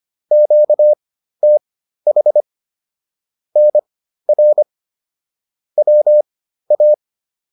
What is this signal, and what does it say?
Morse code, heard as a keyed tone, about 25 wpm, spaced as for 13 wpm, 600 Hz. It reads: QTH NR WA